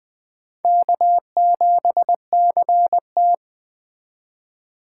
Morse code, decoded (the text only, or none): K7CT